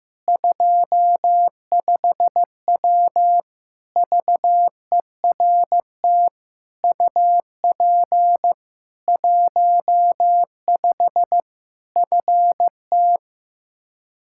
Morse code, heard as a keyed tone, about 15 wpm, 700 Hz. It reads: 25W VERT UP 15 FT
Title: K1USN SST QSO Practice 15wpm